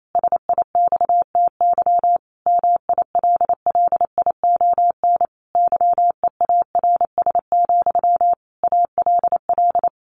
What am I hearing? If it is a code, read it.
Morse code, heard as a keyed tone, about 28 wpm, 700 Hz. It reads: SIXTY MILLION YEARS, ALL
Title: A Sound Of Thunder 28wpm